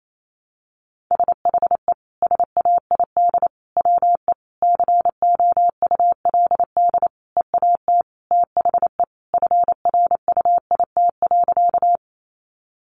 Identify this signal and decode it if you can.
Morse code, heard as a keyed tone, about 28 wpm, 700 Hz. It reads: SHE SAID WE COULD EAT THE FRUIT.
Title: Sentences from Top 1300 Words 28wpm